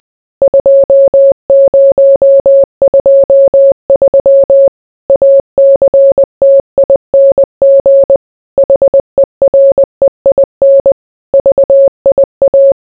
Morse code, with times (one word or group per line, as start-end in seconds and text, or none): 0.4-4.7 2023
5.1-8.2 ACTING
8.6-10.9 HEREIN
11.3-12.7 VIA